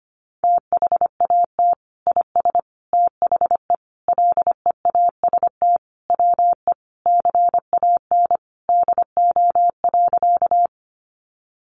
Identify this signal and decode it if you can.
Morse code, heard as a keyed tone, about 25 wpm, 700 Hz. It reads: THAT IS THE LEAST WE CAN DO.
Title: Sentences from Top 700 Words 25wpm